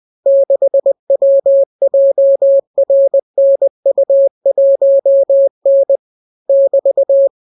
Morse code, heard as a keyed tone, about 20 wpm, 550 Hz. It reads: 6WJRNU1N <BT>